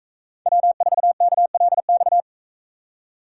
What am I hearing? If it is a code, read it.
Morse code, heard as a keyed tone, about 44 wpm, 700 Hz. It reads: W4KLX